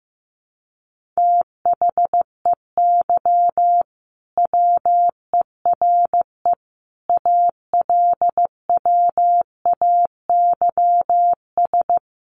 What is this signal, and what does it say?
Morse code, heard as a keyed tone, about 15 wpm, 700 Hz. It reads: THEY WERE ALWAYS
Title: Sentences from Top 1400 Words 15wpm